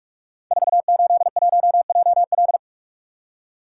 Morse code, heard as a keyed tone, about 44 wpm, 700 Hz. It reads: 481JL
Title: Code Groups - 5 Letter:Numbers - 45wpm